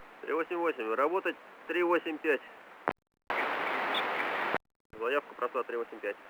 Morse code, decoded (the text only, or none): none